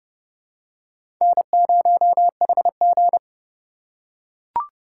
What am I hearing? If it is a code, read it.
Morse code, heard as a keyed tone, about 29 wpm, 700 Hz. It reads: N0HG
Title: USA Call Signs - 4 Characters - 3x Character Spacing - 30wpm